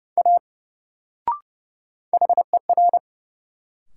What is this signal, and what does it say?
Morse code, heard as a keyed tone, about 29 wpm, 700 Hz. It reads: A HER
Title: Sentences from Top 100 Words - New Word Drill - 30wpm